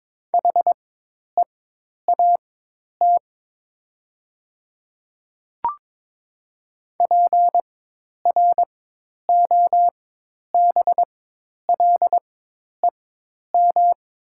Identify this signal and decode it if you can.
Morse code, heard as a keyed tone, about 22 wpm, spaced as for 10 wpm, 700 Hz. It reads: HEAT PROBLEM
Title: 4x Character Spacing - Top 500 Words - 22wpm